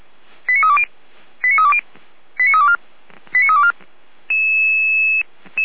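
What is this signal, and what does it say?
Five-tone selective call (ZVEI), heard as quick runs of five tones, tones of about 80 ms. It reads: 89320 89320 89324 89324
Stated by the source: Five-tone ZVEI selective call, received with an SDR, recorded by feuerwehr-orlamuende (fire brigade Orlamünde).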